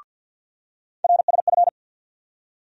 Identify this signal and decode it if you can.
Morse code, heard as a keyed tone, about 52 wpm, 690 Hz. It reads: RSF
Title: USA Call Sign Postfixes - 3 Letters 50wpm